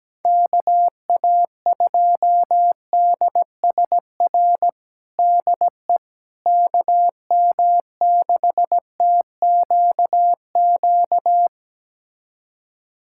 Morse code, heard as a keyed tone, about 17 wpm, 700 Hz. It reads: KA2DSR DE KM6TQQ
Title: USA Callsign DE Callsign 17wpm